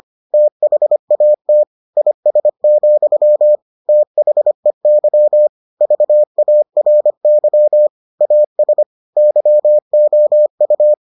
Morse code, heard as a keyed tone, about 25 wpm, 600 Hz. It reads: THAT IS, THEY VARY AS YOU